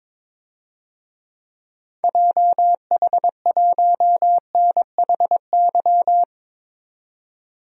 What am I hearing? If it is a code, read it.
Morse code, heard as a keyed tone, about 22 wpm, 700 Hz. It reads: JH1NHY
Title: DXCC-100 Call Signs - 6 Characters - 22wpm